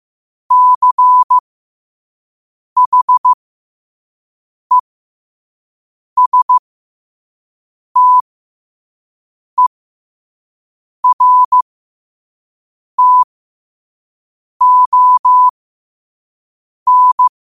Morse code, heard as a keyed tone, about 15 wpm, spaced as for 5 wpm, 1000 Hz. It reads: CHESTERTON
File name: qotdfiles / 5wpm